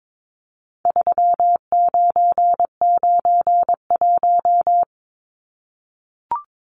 Morse code, heard as a keyed tone, about 22 wpm, 700 Hz. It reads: 3991
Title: Numbers - 4-Digit 22wpm